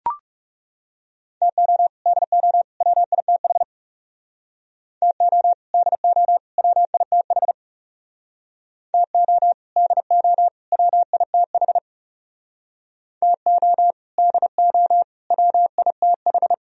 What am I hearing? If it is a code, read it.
Morse code, heard as a keyed tone, about 37 wpm, 700 Hz. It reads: TO DO WITH TO DO WITH TO DO WITH TO DO WITH
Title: Speed-Racing - Sets of 3 Words 30wpm